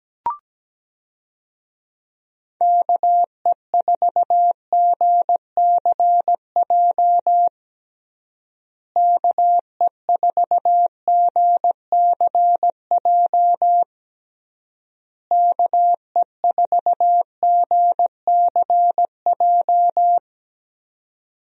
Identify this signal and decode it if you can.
Morse code, heard as a keyed tone, about 17 wpm, 700 Hz. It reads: KE4GCJ KE4GCJ KE4GCJ
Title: USA Call Signs - 6 Characters - Repeated 3x - 17wpm